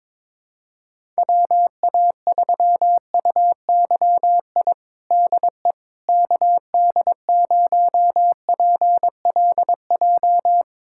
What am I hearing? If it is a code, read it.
Morse code, heard as a keyed tone, about 22 wpm, 700 Hz. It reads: WA3UYI DE KD0PLJ
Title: USA Callsign DE Callsign - 6 Characters - 22wpm